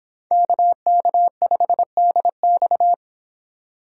Morse code, heard as a keyed tone, about 26 wpm, 700 Hz. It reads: KK5DX